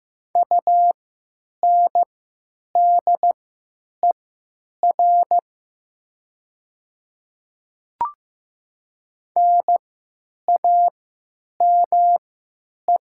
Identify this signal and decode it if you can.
Morse code, heard as a keyed tone, about 15 wpm, spaced as for 9 wpm, 700 Hz. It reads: UNDER NAME
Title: 3x Character Spacing - Top 300 Words - 15wpm